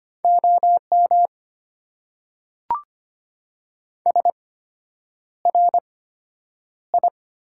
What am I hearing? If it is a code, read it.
Morse code, heard as a keyed tone, about 25 wpm, spaced as for 9 wpm, 700 Hz. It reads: OM SRI
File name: Top 90 QSO Elements - 8x Character Spacing - 25wpm